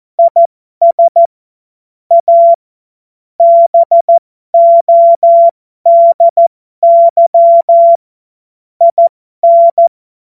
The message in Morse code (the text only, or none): IS A BODY IN